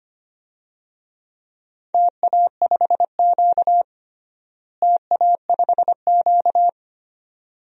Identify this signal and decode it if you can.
Morse code, heard as a keyed tone, about 25 wpm, 700 Hz. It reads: TA5Q TA5Q
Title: DXCC-100 Call Signs - 4 Characters - Repeated 2x - 25wpm